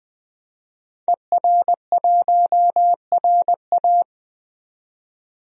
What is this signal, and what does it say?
Morse code, heard as a keyed tone, about 20 wpm, 700 Hz. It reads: ER1RA
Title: DXCC-100 Call Signs - 5 Characters - 20wpm